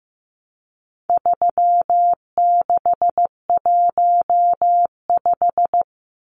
Morse code, heard as a keyed tone, about 15 wpm, 700 Hz. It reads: 3615
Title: Numbers - 4-Digit 15wpm